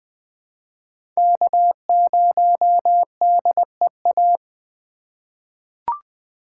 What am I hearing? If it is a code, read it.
Morse code, heard as a keyed tone, about 20 wpm, 700 Hz. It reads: K0DEA
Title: USA Call Signs - 5 Characters - 2x Character Spacing - 20wpm